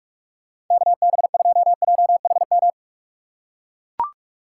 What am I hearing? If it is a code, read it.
Morse code, heard as a keyed tone, about 44 wpm, 700 Hz. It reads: KB2JHM
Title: USA Call Signs - 6 Characters - 4x Character Spacing - 45wpm